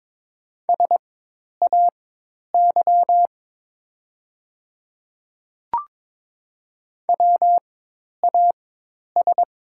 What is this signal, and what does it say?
Morse code, heard as a keyed tone, about 22 wpm, spaced as for 10 wpm, 700 Hz. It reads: SAY WAS
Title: 4x Character Spacing - Top 200 Words - 22wpm